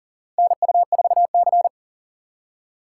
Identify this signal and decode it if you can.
Morse code, heard as a keyed tone, about 40 wpm, 700 Hz. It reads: NU4C
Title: USA Call Signs - 4 Characters - 3x Character Spacing - 40wpm